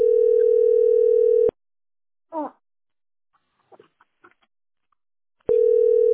Ringback tone (North America), 440 + 480 Hz.